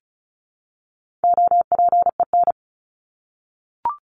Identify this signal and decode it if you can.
Morse code, heard as a keyed tone, about 34 wpm, 700 Hz. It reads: OPEN
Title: Top 300 Words 35wpm